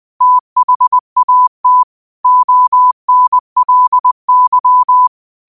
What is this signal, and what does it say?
Morse code, heard as a keyed tone, about 20 wpm, 1000 Hz. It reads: THAT ONLY
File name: qotdfiles / 20wpm